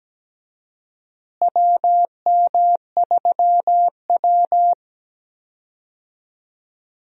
Morse code, heard as a keyed tone, about 17 wpm, 700 Hz. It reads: WM3W
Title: USA Call Signs - 4 Characters - Repeated 2x - 17wpm